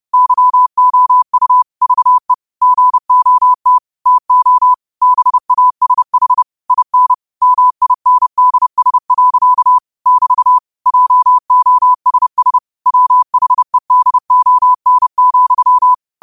Morse code, heard as a keyed tone, about 30 wpm, 1000 Hz. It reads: YOU VE GOT TO BASH IN MINDS. <BT> JOSS WHEDON,
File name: qotdfiles / 30wpm